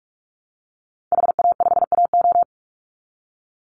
Morse code, heard as a keyed tone, about 44 wpm, 700 Hz. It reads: HA5AO
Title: Call Signs Repeated 3x - 45wpm